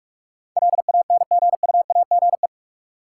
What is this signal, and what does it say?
Morse code, heard as a keyed tone, about 44 wpm, 700 Hz. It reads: LANGUAGE
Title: Sentences from Top 500 Words - New Word Drill - 45wpm